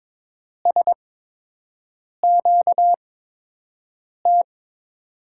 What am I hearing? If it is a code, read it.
Morse code, heard as a keyed tone, about 22 wpm, spaced as for 6 wpm, 700 Hz. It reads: SQT
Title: USA Call Signs - 6 Characters - 8x Character Spacing - 22wpm